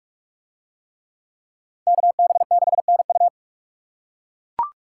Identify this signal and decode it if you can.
Morse code, heard as a keyed tone, about 42 wpm, 700 Hz. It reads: KB6NU